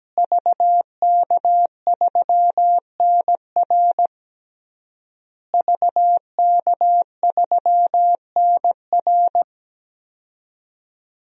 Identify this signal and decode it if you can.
Morse code, heard as a keyed tone, about 17 wpm, 700 Hz. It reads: VK3NR VK3NR
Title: DXCC-100 Call Signs - 5 Characters - Repeated 3x - 17wpm